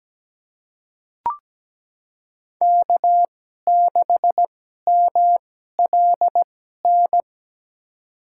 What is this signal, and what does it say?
Morse code, heard as a keyed tone, about 17 wpm, spaced as for 12 wpm, 700 Hz. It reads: K6MLN
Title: USA Call Signs - 5 Characters - 2x Character Spacing - 17wpm